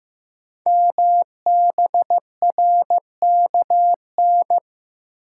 Morse code, heard as a keyed tone, about 15 wpm, 690 Hz. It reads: MBRKN
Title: Code Groups - 5 Letters - 15wpm